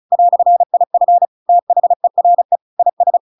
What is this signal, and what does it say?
Morse code, heard as a keyed tone, about 34 wpm, 700 Hz. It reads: "IF THERE IS
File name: Pinocchio / Pinocchio ch7 35 35 ik1hge